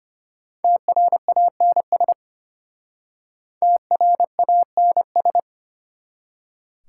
Morse code, heard as a keyed tone, about 32 wpm, 700 Hz. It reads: TRANS TRANS